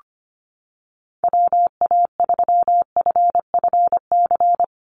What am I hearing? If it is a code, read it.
Morse code, heard as a keyed tone, about 25 wpm, 700 Hz. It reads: WA3FFC
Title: USA Call Signs 25wpm